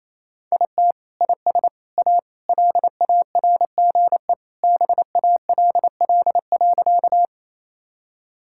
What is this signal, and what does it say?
Morse code, heard as a keyed tone, about 27 wpm, 700 Hz. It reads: IT IS A LARGE BALL.
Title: Sentences from Top 1000 Words - Review Part 6 - 28wpm